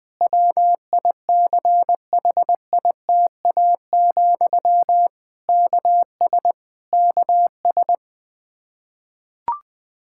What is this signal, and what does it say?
Morse code, heard as a keyed tone, about 20 wpm, 700 Hz. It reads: WICHITA, KS KS
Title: Practice QSO - Exchange 1 - Beginning - 20wpm